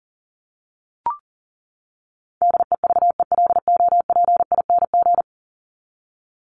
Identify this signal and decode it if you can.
Morse code, heard as a keyed tone, about 40 wpm, 700 Hz. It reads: DEVELOPING
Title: ING Suffix - Top 100 Words with Base Words 40wpm